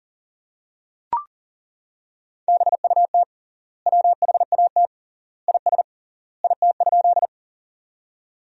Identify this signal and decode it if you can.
Morse code, heard as a keyed tone, about 40 wpm, 700 Hz. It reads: BUT WHAT IS IT?
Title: 3x Word-Spacing - Sentences from Top 100 Words - 40wpm